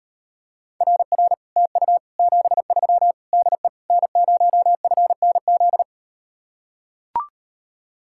Morse code, heard as a keyed tone, about 38 wpm, 700 Hz. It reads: RR TU 73 DE N0FNZ